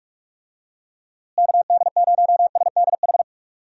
Morse code, heard as a keyed tone, about 44 wpm, 700 Hz. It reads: KD0SDH
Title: USA Call Signs - 6 Characters - Repeated 3x - 45wpm